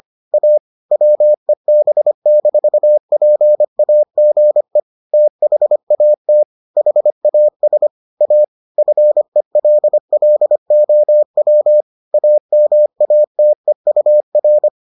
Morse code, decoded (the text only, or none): A WEB-PAGE THAT HAS A FELLOW AMATEUR